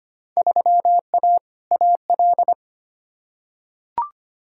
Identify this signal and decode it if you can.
Morse code, heard as a keyed tone, about 25 wpm, 700 Hz. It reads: 3A AL
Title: Field Day Exchanges - 25wpm